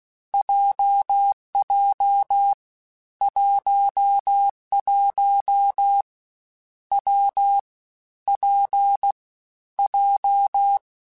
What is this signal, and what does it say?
Morse code, heard as a keyed tone, about 16 wpm, 800 Hz. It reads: JJ 11 W P J
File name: Difficult Combos / Difficult Combos-mp3s / WPJ WPJ1 @15